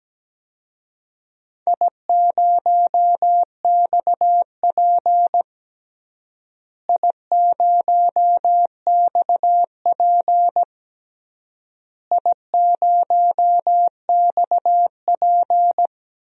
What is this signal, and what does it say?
Morse code, heard as a keyed tone, about 17 wpm, 700 Hz. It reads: I0XP I0XP I0XP
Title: DXCC-100 Call Signs - 4 Characters - Repeated 3x - 17wpm